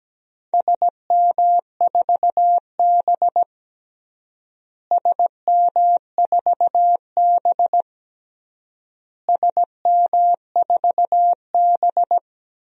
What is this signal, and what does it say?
Morse code, heard as a keyed tone, about 17 wpm, 700 Hz. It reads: SM4B SM4B SM4B
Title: DXCC-100 Call Signs - 4 Characters - Repeated 3x - 17wpm